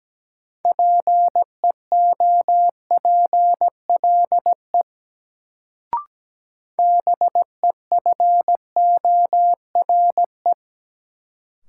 Morse code, heard as a keyed tone, about 17 wpm, 700 Hz. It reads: PEOPLE BEFORE
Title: Sets of 2 Words - New Word Drill - 17wpm